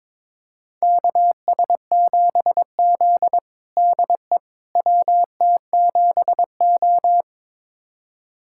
Morse code, heard as a keyed tone, about 22 wpm, 700 Hz. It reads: KS7Z DE WT7O
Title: USA Callsign DE Callsign - 4 Characters - 22wpm